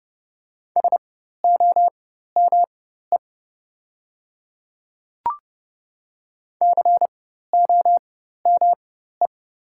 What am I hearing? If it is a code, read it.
Morse code, heard as a keyed tone, about 30 wpm, spaced as for 14 wpm, 700 Hz. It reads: SOME COME
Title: 4x Character Spacing - Top 100 Words - 30wpm